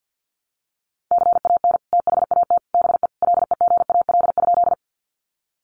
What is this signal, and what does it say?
Morse code, heard as a keyed tone, about 50 wpm, 700 Hz. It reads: CAN THAT BE LEGAL?
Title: Sentences from Top 1200 Words 50wpm